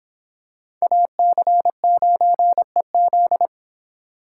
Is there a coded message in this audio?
Yes: Morse code from 0.8 to 3.5 s, heard as a keyed tone, about 26 wpm, 700 Hz. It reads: AC9EZ